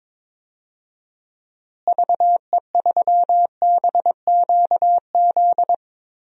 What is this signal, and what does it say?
Morse code, heard as a keyed tone, about 22 wpm, 700 Hz. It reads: VE3BQZ
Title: DXCC-100 Call Signs - 6 Characters - Repeated 3x - 22wpm